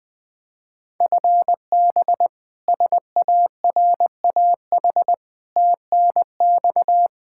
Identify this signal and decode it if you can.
Morse code, heard as a keyed tone, about 21 wpm, 700 Hz. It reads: FB SARAH TNX
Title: K1USN SST QSO Practice 20wpm